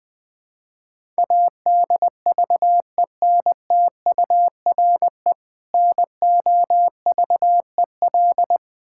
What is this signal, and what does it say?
Morse code, heard as a keyed tone, about 20 wpm, 700 Hz. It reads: ADVENTURE NOVEL
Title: Difficult Words with Hints - 20wpm